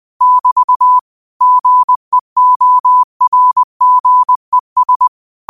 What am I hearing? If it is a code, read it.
Morse code, heard as a keyed tone, about 20 wpm, 1000 Hz. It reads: <BT> GEORGES
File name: qotdfiles / 20wpm